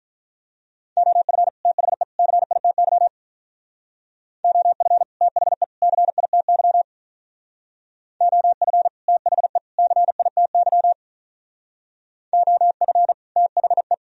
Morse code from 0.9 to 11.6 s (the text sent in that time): OF THE CITY OF THE CITY OF THE CITY